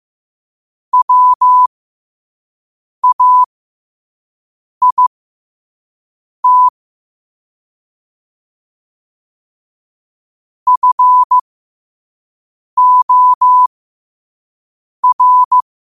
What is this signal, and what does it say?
Morse code, heard as a keyed tone, about 15 wpm, spaced as for 5 wpm, 1000 Hz. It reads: WAIT FOR
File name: qotdfiles / 5wpm